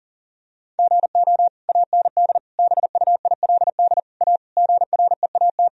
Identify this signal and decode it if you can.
Morse code, heard as a keyed tone, about 37 wpm, 700 Hz. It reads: GO AND BUILD A GREAT
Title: Sentences from Top 1000 Words - Review Part 1 - 40wpm